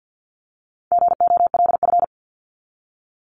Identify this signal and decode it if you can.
Morse code, heard as a keyed tone, about 50 wpm, 700 Hz. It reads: GOLF
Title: Top 1751-2000 Words 50wpm